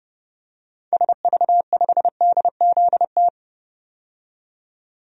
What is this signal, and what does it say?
Morse code, heard as a keyed tone, about 29 wpm, 690 Hz. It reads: SV5DZT
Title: DXCC-100 Call Signs - 6 Characters - 30wpm